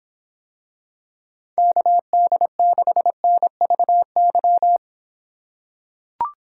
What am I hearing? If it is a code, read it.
Morse code, heard as a keyed tone, about 26 wpm, 700 Hz. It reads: KD6NVY